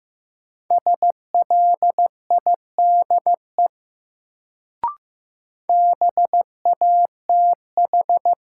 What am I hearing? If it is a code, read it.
Morse code, heard as a keyed tone, about 15 wpm, 700 Hz. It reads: SLIDE BATH
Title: Sentences from Top 1600 Words - New Word Drill - 15wpm